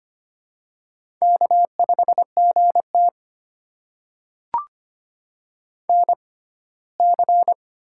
Morse code, heard as a keyed tone, about 25 wpm, spaced as for 17 wpm, 700 Hz. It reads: K5GT NC
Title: USA Call Signs - 6x Character Spacing - 25wpm